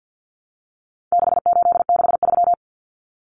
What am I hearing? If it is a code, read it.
Morse code, heard as a keyed tone, about 48 wpm, 700 Hz. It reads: 6863
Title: Common Cut Numbers - 4-Digit 50wpm